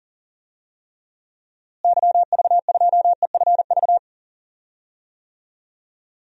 Morse code, heard as a keyed tone, about 37 wpm, 700 Hz. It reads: YV2EFV
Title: DXCC-100 Call Signs - 6 Characters - Repeated 2x - 40wpm